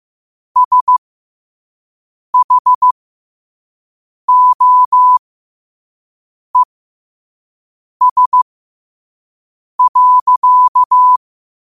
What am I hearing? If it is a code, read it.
Morse code, heard as a keyed tone, about 15 wpm, spaced as for 5 wpm, 1000 Hz. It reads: SHOES.
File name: 5wpm